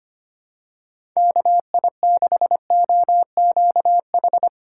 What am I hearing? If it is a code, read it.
Morse code, heard as a keyed tone, about 25 wpm, 700 Hz. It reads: KI6OQH
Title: USA Call Signs - 2x Character Spacing - 25wpm